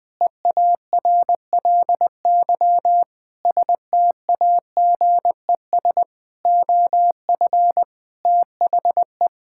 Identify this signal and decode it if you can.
Morse code, heard as a keyed tone, about 20 wpm, 700 Hz. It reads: EARLY STAGES OF THE
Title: Sets of 6 Words - 20wpm